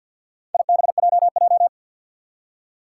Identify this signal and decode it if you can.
Morse code, heard as a keyed tone, about 52 wpm, 700 Hz. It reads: IBJJ
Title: Four Challenging Letter-Number Combinations 50wpm